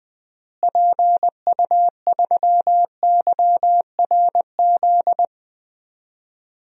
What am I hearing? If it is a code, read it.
Morse code, heard as a keyed tone, about 20 wpm, 700 Hz. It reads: PU3YRZ
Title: DXCC-100 Call Signs - 6 Characters - Repeated 3x - 20wpm